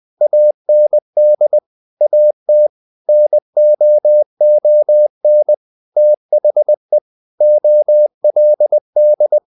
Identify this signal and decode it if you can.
Morse code, heard as a keyed tone, about 20 wpm, 600 Hz. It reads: AND AT NOON THE OLD
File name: the old man and the sea / the old man and the sea part 1 20wpm